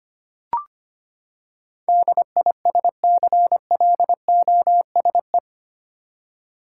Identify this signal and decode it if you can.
Morse code, heard as a keyed tone, about 25 wpm, 700 Hz. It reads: DISCLOSE
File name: Eight Letter Words - 25wpm